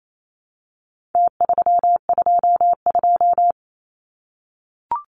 Morse code, heard as a keyed tone, about 28 wpm, 700 Hz. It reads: T322
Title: Numbers - Contest Style - 4-Digit 28wpm